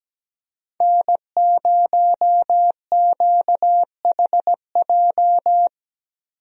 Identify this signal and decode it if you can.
Morse code, heard as a keyed tone, about 17 wpm, 700 Hz. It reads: N0QHJ
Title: USA Call Signs - 5 Characters - Repeated 2x - 17wpm